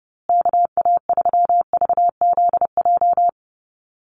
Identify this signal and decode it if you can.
Morse code, heard as a keyed tone, about 30 wpm, 700 Hz. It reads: KA3VZJ
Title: USA Call Signs Repeated 3x 30wpm